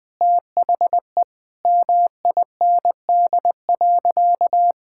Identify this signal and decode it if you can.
Morse code, heard as a keyed tone, about 21 wpm, 700 Hz. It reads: THE MIND.
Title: Sentences with 4-Word Phrases - 20wpm